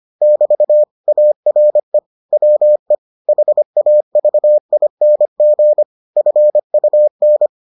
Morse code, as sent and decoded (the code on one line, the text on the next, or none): -...- / .- .-. / .-- . / .... .- ...- .. -. --. / ..-. ..- -.
<BT> ARE WE HAVING FUN